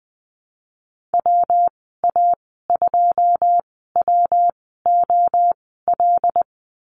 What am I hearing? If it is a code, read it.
Morse code, heard as a keyed tone, about 19 wpm, spaced as for 14 wpm, 700 Hz. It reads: WA2WOL
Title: 140 Lesson 23g - Letter f - Call Signs 20wpm